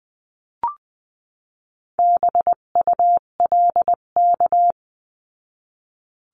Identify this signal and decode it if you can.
Morse code, heard as a keyed tone, about 20 wpm, 700 Hz. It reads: BULK
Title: 258 Lesson 42c - Words 16-20wpm